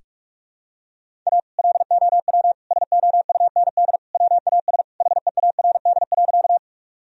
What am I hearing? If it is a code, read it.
Morse code, heard as a keyed tone, about 41 wpm, 700 Hz. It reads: A LOW SOUND WAS HEARD.
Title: Sentences from Top 1000 Words - Review Part 1 - 45wpm